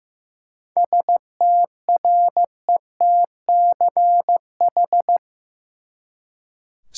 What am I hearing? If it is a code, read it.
Morse code, heard as a keyed tone, about 16 wpm, 700 Hz. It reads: STRETCH
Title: Seven Letter Words - 15wpm